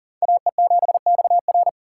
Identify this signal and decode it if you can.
Morse code, heard as a keyed tone, about 51 wpm, 700 Hz. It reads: AE7XR